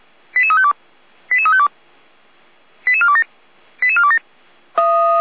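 Five-tone selective call (ZVEI), heard as quick runs of five tones, tones of about 80 ms. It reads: 88352 88352 88427 88427
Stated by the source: ZVEI five-tone selective call received on SDR, fire brigade Feuerwehr Orlamünde.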